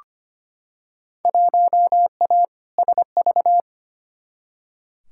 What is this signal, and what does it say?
Morse code, heard as a keyed tone, about 25 wpm, 700 Hz. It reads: 1A SV
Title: Field Day Exchanges Encore - 25wpm